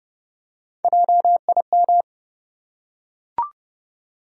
Morse code, heard as a keyed tone, about 30 wpm, 700 Hz. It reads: JIM
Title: USA Common Names - Max 4 Letters - 30wpm